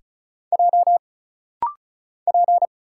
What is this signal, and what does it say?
Morse code, heard as a keyed tone, about 34 wpm, 700 Hz. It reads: J P
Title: Single Letter-Number Alt - Rapid-Fire 35wpm